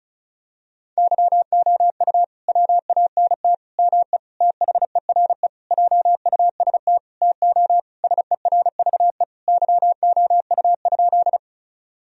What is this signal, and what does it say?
Morse code, heard as a keyed tone, about 36 wpm, 700 Hz. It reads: YOU WANT ME THERE JUST TO SERVE YOU?
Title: Sentences from Top 1000 Words - Review Part 2 - 35wpm